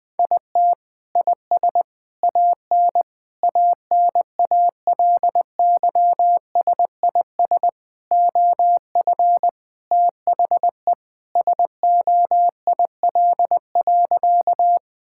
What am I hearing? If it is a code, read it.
Morse code, heard as a keyed tone, about 19 wpm, 700 Hz. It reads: IT IS AN ANALYSIS OF THE SOIL.
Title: Sentences from Top 1000 Words - Review Part 5 - 20wpm